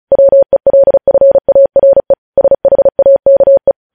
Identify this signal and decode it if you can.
Morse code, heard as a keyed tone, about 36 wpm, 550 Hz. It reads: WELFARE SHAKE
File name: podcast / News headlines-1x-35x35